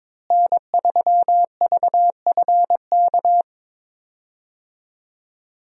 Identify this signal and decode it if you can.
Morse code, heard as a keyed tone, about 22 wpm, 700 Hz. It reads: N3VFK
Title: USA Call Signs - 5 Characters - Repeated 2x - 22wpm